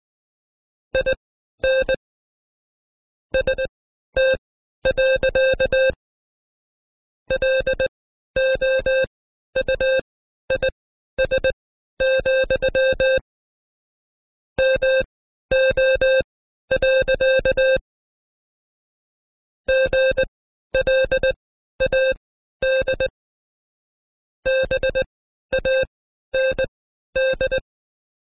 Morse code, heard as a keyed tone, about 20 wpm, spaced as for 12 wpm, 530 Hz. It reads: IN ST. LOUIS, MO. GLAD BAND
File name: Code Quick / AudioDownld2 / Seg132-Stepper5 11WPM